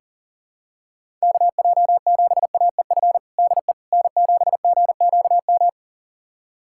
Morse code, heard as a keyed tone, about 40 wpm, 700 Hz. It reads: KJ7AEF DE N7GQM
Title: USA Callsign DE Callsign 40wpm